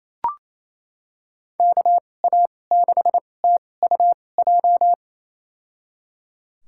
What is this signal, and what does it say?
Morse code, heard as a keyed tone, about 28 wpm, spaced as for 20 wpm, 700 Hz. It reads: KA6TUJ
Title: USA Call Signs - 2x Character Spacing - 28wpm